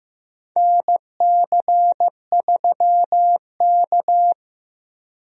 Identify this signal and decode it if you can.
Morse code, heard as a keyed tone, about 15 wpm, 700 Hz. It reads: NC3K
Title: USA Call Signs - 4 Characters - 15wpm